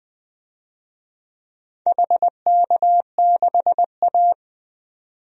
Morse code, heard as a keyed tone, about 20 wpm, 700 Hz. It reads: HK6A
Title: DXCC-100 Call Signs - 4 Characters - 20wpm